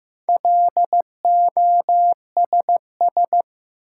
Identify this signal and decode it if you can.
Morse code, heard as a keyed tone, about 15 wpm, 700 Hz. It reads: LOSS
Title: Sentences from Top 1400 Words - New Word Drill Alt - 15wpm